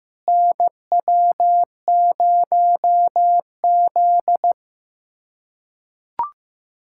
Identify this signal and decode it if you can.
Morse code, heard as a keyed tone, about 16 wpm, 700 Hz. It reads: NW0Z
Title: USA Callsign DE Callsign - 4 Characters - 15wpm